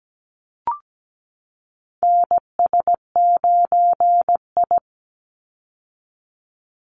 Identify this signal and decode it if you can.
Morse code, heard as a keyed tone, about 17 wpm, 700 Hz. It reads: NS9I
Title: Call Signs - 17wpm